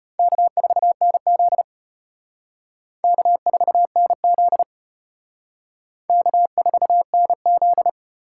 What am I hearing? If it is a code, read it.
Morse code, heard as a keyed tone, about 31 wpm, 700 Hz. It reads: K4NZ K4NZ K4NZ
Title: Speed Racing - USA 1x2 Call Signs 25wpm